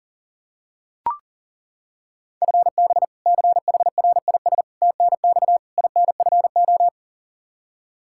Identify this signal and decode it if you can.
Morse code, heard as a keyed tone, about 40 wpm, 700 Hz. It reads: FB CHRIS TNX INFO
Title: Practice QSO - Exchange 3 - More Info - 40wpm